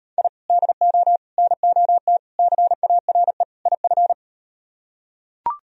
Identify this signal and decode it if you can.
Morse code, heard as a keyed tone, about 39 wpm, 700 Hz. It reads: I DO NOT CARE IF